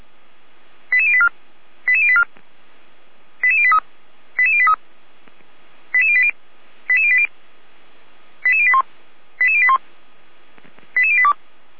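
Five-tone selective call (ZVEI), heard as quick runs of five tones, tones of about 70 ms. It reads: 88084 88084 88083 88083 88080 88080 88081 88081 88082
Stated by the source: ZVEI five-tone selective call received on an SDR.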